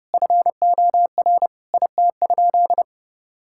Morse code, heard as a keyed tone, about 29 wpm, 700 Hz. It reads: FOR IT?
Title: Sentences from Top 1000 Words - Review Part 6 - 30wpm